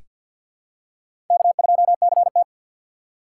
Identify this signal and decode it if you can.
Morse code, heard as a keyed tone, about 46 wpm, 700 Hz. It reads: K2CT
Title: USA Call Signs - 4 Characters - 2x Character Spacing - 50wpm